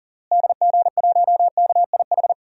Morse code, heard as a keyed tone, about 40 wpm, 700 Hz. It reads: DG1KIH
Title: DXCC-100 Call Signs - 6 Characters - Repeated 3x - 40wpm